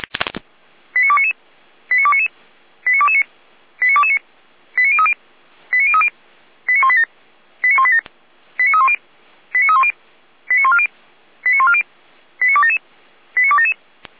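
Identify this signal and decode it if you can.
Five-tone selective call (ZVEI), heard as quick runs of five tones, tones of about 70 ms. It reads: 89299 89299 89229 89229 89030 89030 89187 89187 89310 89310 89140 89140 89277 89277
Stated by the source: ZVEI five-tone selective call 89165 received by SDR.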